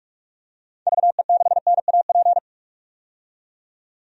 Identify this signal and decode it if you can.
Morse code, heard as a keyed tone, about 46 wpm, 700 Hz. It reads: VE6NAP
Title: DXCC-100 Call Signs - 6 Characters - 45wpm